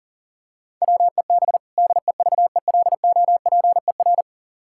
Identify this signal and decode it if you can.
Morse code, heard as a keyed tone, about 40 wpm, 700 Hz. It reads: WEB DEVELOPER
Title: K1USN SST QSO Practice 40wpm